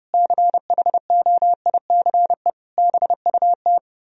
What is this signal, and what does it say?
Morse code, heard as a keyed tone, about 39 wpm, 700 Hz. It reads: CHOICE BUT